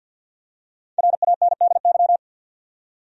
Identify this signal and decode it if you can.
Morse code, heard as a keyed tone, about 51 wpm, 700 Hz. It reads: RANDY
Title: USA Common Names - Max 5 Letters - 50wpm